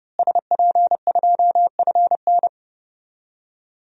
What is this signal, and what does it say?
Morse code, heard as a keyed tone, about 32 wpm, 700 Hz. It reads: SP2FN
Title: DXCC-100 Call Signs - 5 Characters - Repeated 2x - 30wpm